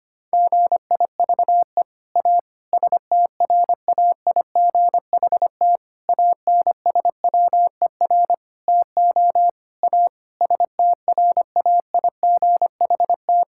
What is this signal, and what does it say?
Morse code, heard as a keyed tone, about 25 wpm, 700 Hz. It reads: GIVE A STRAIGHT ANSWER TO A STRAIGHT
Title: Sentences from Top 1000 Words - Review Part 6 - 25wpm